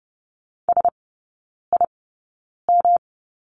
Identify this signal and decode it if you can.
Morse code, heard as a keyed tone, about 30 wpm, spaced as for 9 wpm, 700 Hz. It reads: SIM
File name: Lesson 18e - Letter m - Three Characters 30wpm